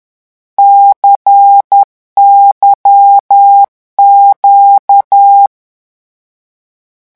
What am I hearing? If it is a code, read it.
Morse code, heard as a keyed tone, about 10 wpm, 800 Hz. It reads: CYQ
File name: morse mp3 / lecon2-vitesse 10